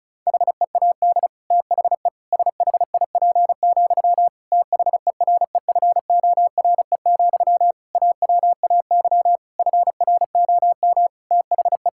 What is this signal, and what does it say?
Morse code, heard as a keyed tone, about 34 wpm, 700 Hz. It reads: HEAD THE SHIP, THEREFORE, AWAY FROM THE
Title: Sentences from Top 1000 Words - Review Part 5 - 35wpm